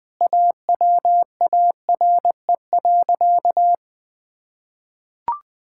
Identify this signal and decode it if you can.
Morse code, heard as a keyed tone, about 20 wpm, 700 Hz. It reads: AWARE.
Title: Sentences with 3-Word Phrases - 20wpm